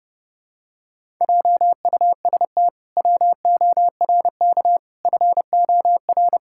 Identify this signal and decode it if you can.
Morse code, heard as a keyed tone, about 30 wpm, 700 Hz. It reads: JUST WORK FOR
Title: Pangram Challenge v2 - 30wpm